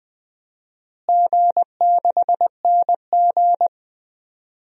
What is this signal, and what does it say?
Morse code, heard as a keyed tone, about 20 wpm, 700 Hz. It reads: G6NG
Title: DXCC-100 Call Signs - 4 Characters - Repeated 3x - 20wpm